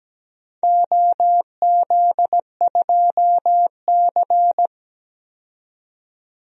Morse code, heard as a keyed tone, about 17 wpm, 700 Hz. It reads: OZ2C
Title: DXCC-100 Call Signs - 4 Characters - 17wpm